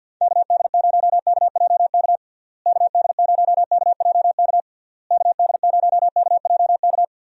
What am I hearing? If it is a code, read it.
Morse code, heard as a keyed tone, about 52 wpm, 700 Hz. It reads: KD0KJK KD0KJK KD0KJK
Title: USA Call Signs - 6 Characters - Repeated 3x - 50wpm